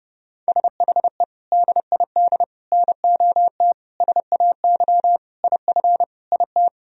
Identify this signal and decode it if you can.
Morse code, heard as a keyed tone, about 30 wpm, 700 Hz. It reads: SHE DID NOT SAY IF IT
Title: Sentences from Top 1000 Words - Review Part 3 - 30wpm